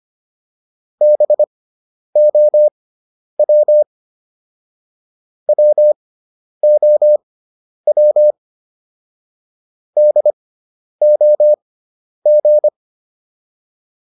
Morse code, heard as a keyed tone, about 25 wpm, spaced as for 10 wpm, 600 Hz. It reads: BOW WOW DOG